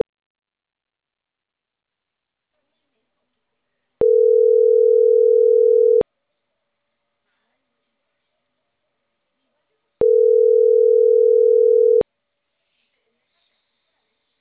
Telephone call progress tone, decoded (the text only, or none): ringback tone